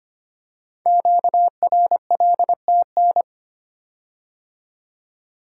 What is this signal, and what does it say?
Morse code, heard as a keyed tone, about 25 wpm, 700 Hz. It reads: QRLTN
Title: Code Groups - 5 Letters - 25wpm